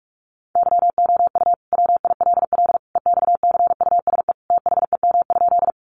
Morse code, heard as a keyed tone, about 44 wpm, 700 Hz. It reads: YOU WILL EXCUSE THEM?
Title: Sentences from Top 700 Words 45wpm